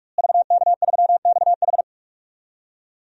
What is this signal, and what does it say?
Morse code, heard as a keyed tone, about 46 wpm, 700 Hz. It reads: VK3XH
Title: DXCC-100 Call Signs - 5 Characters - Repeated 3x - 45wpm